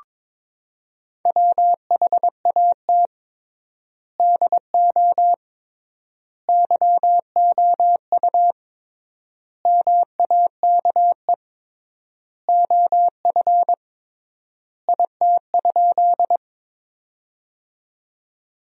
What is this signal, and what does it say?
Morse code, heard as a keyed tone, about 22 wpm, 700 Hz. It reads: WHAT DO YOU MAKE OF IT?
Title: 3x Word-Spacing - Sentences from Top 100 Words - 22wpm